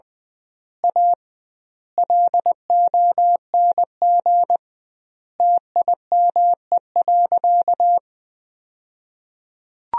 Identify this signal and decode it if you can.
Morse code, heard as a keyed tone, about 20 wpm, 700 Hz. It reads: A LONG TIME.
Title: 2x Word-Spacing - Sentences from Top 400 Words - 20wpm